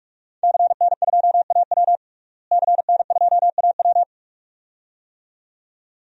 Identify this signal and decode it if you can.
Morse code, heard as a keyed tone, about 44 wpm, 700 Hz. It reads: CN2AW CN2AW
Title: DXCC-100 Call Signs - 5 Characters - Repeated 2x - 45wpm